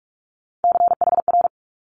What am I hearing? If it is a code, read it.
Morse code, heard as a keyed tone, about 46 wpm, 700 Hz. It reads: CHR
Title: Multi-letter Phonemes Quick Fire 45wpm